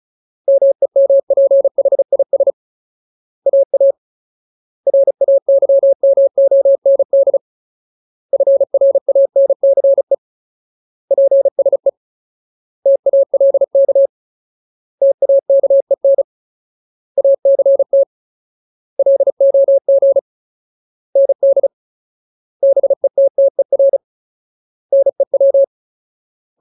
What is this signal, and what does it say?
Morse code, heard as a keyed tone, about 35 wpm, 550 Hz. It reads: MEMPHIS AA RAYMOND FRANCE PSE TALK TAKEN ACT LOG ND BETTER NEW
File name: Ultimate Word List / 35WPM Ultimate